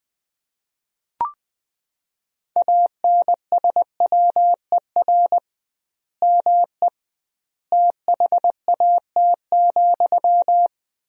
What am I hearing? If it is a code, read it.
Morse code, heard as a keyed tone, about 20 wpm, 700 Hz. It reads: ANSWER ME THAT,
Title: 2x Word-Spacing - Sentences from Top 500 Words - 20wpm